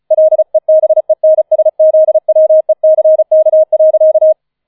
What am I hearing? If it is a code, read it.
Morse code, heard as a keyed tone, about 34 wpm, 630 Hz. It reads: LEBENSZWECK.